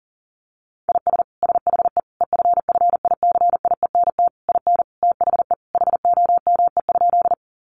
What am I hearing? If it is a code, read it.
Morse code, heard as a keyed tone, about 40 wpm, 700 Hz. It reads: IS SHE EFFICIENT IN THE HOME?
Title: Sentences from Top 1300 Words 40wpm